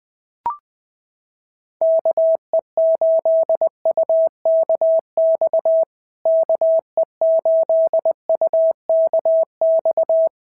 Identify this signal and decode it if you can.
Morse code, heard as a keyed tone, about 20 wpm, 650 Hz. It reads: KE8UKX KE8UKX
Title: POTA QSO Practice - 20wpm